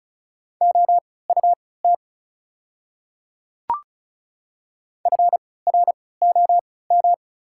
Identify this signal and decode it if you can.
Morse code, heard as a keyed tone, about 36 wpm, spaced as for 20 wpm, 690 Hz. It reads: OUT FROM
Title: 3x Character Spacing - Top 100 Words - 35wpm